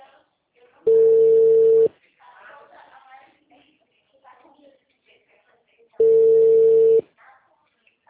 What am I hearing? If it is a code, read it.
Ringback tone (China, 450 Hz), with speech.